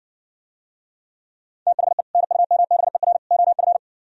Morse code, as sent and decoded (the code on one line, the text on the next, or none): - .... / -. ..- -- -... .-. / --- ..-.
THE NUMBER OF